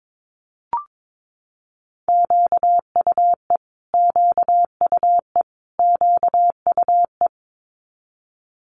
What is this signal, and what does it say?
Morse code, heard as a keyed tone, about 22 wpm, 700 Hz. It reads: QUE QUE QUE
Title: Multi-letter Phonemes Repeated 3x 22wpm